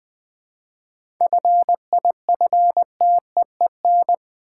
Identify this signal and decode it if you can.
Morse code, heard as a keyed tone, about 20 wpm, 700 Hz. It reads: FIFTEEN
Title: 7 Letter Words - First 3 Letters Given - 20wpm